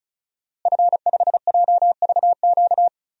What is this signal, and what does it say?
Morse code, heard as a keyed tone, about 36 wpm, 700 Hz. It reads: F5JVQ
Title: DXCC-100 Call Signs - 5 Characters - Repeated 2x - 35wpm